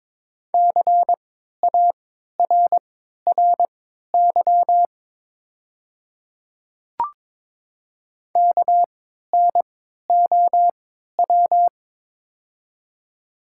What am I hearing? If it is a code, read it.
Morse code, heard as a keyed tone, about 23 wpm, spaced as for 13 wpm, 690 Hz. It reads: CARRY KNOW
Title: 3x Character Spacing - Top 500 Words - 22wpm